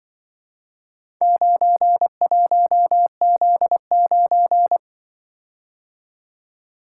Morse code, heard as a keyed tone, about 24 wpm, 700 Hz. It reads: 91Z9
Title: Speed-Racing - Four Letter-Number Combinations - 20wpm